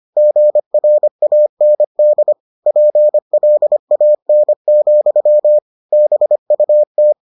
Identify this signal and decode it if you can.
Morse code, heard as a keyed tone, about 25 wpm, 600 Hz. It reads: GRAND PLAN, BUT